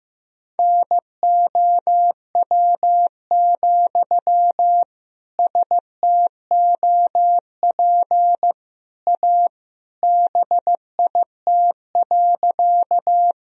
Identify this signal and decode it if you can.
Morse code, heard as a keyed tone, about 15 wpm, 700 Hz. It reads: NOW, STOP A BIT.
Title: Sentences from Top 1000 Words - Review Part 1 - 15wpm